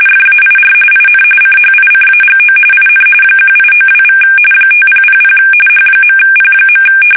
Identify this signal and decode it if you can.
RTTY, about 45.45 baud. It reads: FAIR IN CHICAGO IN 1892= 5000 H.P.
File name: fsk850 / benjamin-g-lamme